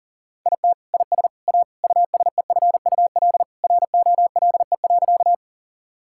Morse code, heard as a keyed tone, about 37 wpm, 700 Hz. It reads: IT IS A USEFUL ROLE.
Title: Sentences from Top 1000 Words - Review Part 6 - 40wpm